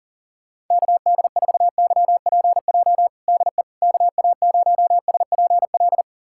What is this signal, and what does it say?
Morse code, heard as a keyed tone, about 40 wpm, 700 Hz. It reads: KD4YPJ DE KA0SPL
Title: USA Callsign DE Callsign - 6 Characters - 40wpm